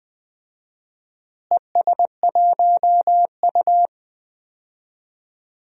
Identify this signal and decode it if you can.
Morse code, heard as a keyed tone, about 20 wpm, 700 Hz. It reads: ES1U
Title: DXCC-100 Call Signs - 4 Characters - Repeated 3x - 20wpm